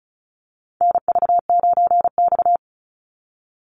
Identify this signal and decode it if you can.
Morse code, heard as a keyed tone, about 34 wpm, 700 Hz. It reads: NV9X
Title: Call Signs Repeated 2x - 35wpm